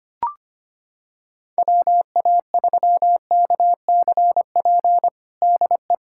Morse code, heard as a keyed tone, about 25 wpm, 700 Hz. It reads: WA3KCP DE
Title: USA Callsign DE Callsign - 6 Characters - 25wpm